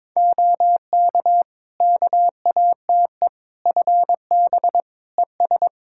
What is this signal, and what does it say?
Morse code, heard as a keyed tone, about 21 wpm, 700 Hz. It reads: OK KATE FB ES
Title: Practice QSO - Exchange 2 - Info - 22wpm